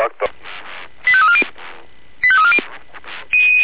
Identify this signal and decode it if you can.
Five-tone selective call (ZVEI), heard as quick runs of five tones, tones of about 70 ms, with speech. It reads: 85340 85340